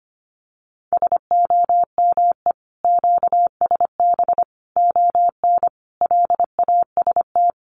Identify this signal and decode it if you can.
Morse code, heard as a keyed tone, about 25 wpm, 700 Hz. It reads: SOME QSB ON LAST
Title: Common QSO Phrases 25wpm